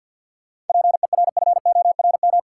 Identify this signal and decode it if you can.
Morse code, heard as a keyed tone, about 52 wpm, 700 Hz. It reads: PERFORM
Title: Sentences from Top 1000 Words - Review Part 5 - New Word Drill - 50wpm